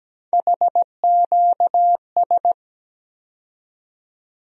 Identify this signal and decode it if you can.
Morse code, heard as a keyed tone, about 17 wpm, 700 Hz. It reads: HQS
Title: USA Call Sign Postfixes - 3 Letters 17wpm